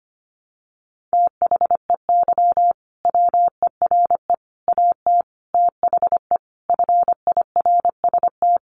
Morse code, heard as a keyed tone, about 25 wpm, 700 Hz. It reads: THEY WERE AT THE FIRST
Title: Sentences from Top 1200 Words 25wpm